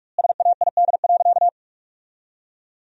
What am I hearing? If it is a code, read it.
Morse code, heard as a keyed tone, about 46 wpm, 700 Hz. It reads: SAID.
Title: 2x Word-Spacing - Sentences from Top 200 Words - 45wpm